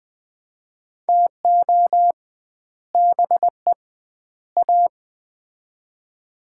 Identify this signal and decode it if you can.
Morse code, heard as a keyed tone, about 20 wpm, 700 Hz. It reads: TO BE A
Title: 2x Word-Spacing - Sets of 3 Words - 20wpm